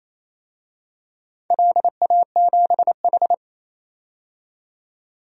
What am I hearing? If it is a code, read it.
Morse code, heard as a keyed tone, about 28 wpm, 700 Hz. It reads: LA7H
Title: DXCC-100 Call Signs - 4 Characters - 28wpm